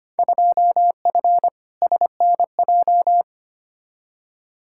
Morse code, heard as a keyed tone, about 25 wpm, 700 Hz. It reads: 2F SNJ
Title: Field Day Exchanges Encore - 25wpm